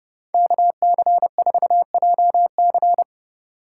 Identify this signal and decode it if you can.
Morse code, heard as a keyed tone, about 31 wpm, 700 Hz. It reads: KC4JC
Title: USA Call Signs - 5 Characters - 3x Character Spacing - 30wpm